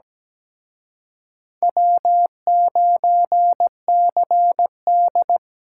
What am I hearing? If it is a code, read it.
Morse code, heard as a keyed tone, about 20 wpm, 700 Hz. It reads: W9CD
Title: Speed Racing - USA Call Signs 17wpm